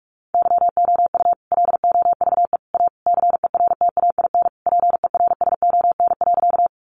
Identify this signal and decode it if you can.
Morse code, heard as a keyed tone, about 44 wpm, 700 Hz. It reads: YOU LOVE A CERTAIN PERSON.
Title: Sentences from Top 500 Words 45wpm